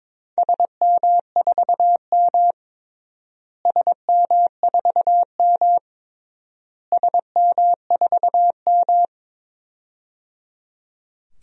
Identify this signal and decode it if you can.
Morse code, heard as a keyed tone, about 22 wpm, 700 Hz. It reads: SM4M SM4M SM4M
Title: DXCC-100 Call Signs - 4 Characters - Repeated 3x - 22wpm